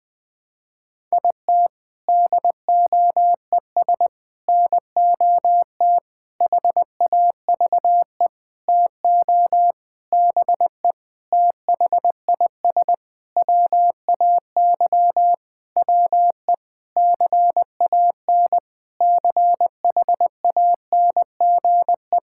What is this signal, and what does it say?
Morse code, heard as a keyed tone, about 21 wpm, 700 Hz. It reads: IT DOES NOT HAVE TO BE THIS WAY WE CAN CHANGE
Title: Sentences with 7-Word Phrases - Spoken Beginning - 20wpm